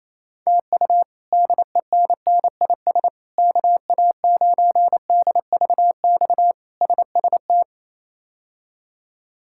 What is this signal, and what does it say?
Morse code, heard as a keyed tone, about 28 wpm, 700 Hz. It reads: TU DENNIS KA9DVX SST
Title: K1USN SST QSO Practice - 28wpm